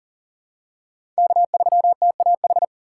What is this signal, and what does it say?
Morse code, heard as a keyed tone, about 40 wpm, 700 Hz. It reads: K3TAH
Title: USA Call Signs - 5 Characters - 3x Character Spacing - 40wpm